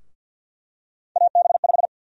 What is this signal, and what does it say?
Morse code, heard as a keyed tone, about 51 wpm, 700 Hz. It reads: AB5
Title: USA Call Sign Prefixes - Repeated 2x - 50wpm